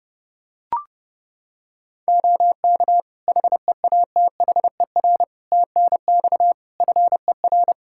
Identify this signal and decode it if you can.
Morse code, heard as a keyed tone, about 30 wpm, 700 Hz. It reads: OK HEATHER TNX FER
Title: K1USN SST QSO Practice 30wpm